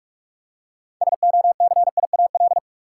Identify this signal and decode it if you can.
Morse code, heard as a keyed tone, about 46 wpm, 700 Hz. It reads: SOCIAL
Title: Practice QSO - Exchange 3 - More Info - 45wpm